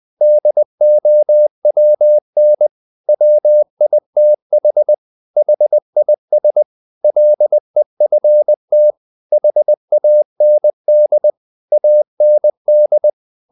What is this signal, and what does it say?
Morse code, heard as a keyed tone, about 20 wpm, 600 Hz. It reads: DOWN WITH HIS LEFT HAND AND
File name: the old man and the sea / the old man and the sea part 1 20wpm